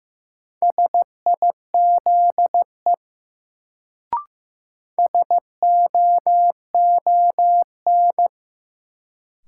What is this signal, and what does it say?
Morse code, heard as a keyed tone, about 15 wpm, 700 Hz. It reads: SIZE SOON
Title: Sets of 3 Words - New Word Drill - 15wpm